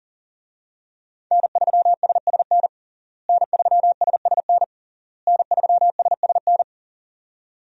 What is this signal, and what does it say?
Morse code, heard as a keyed tone, about 40 wpm, 700 Hz. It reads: N3SSN N3SSN N3SSN
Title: USA Call Signs - 5 Characters - Repeated 3x - 40wpm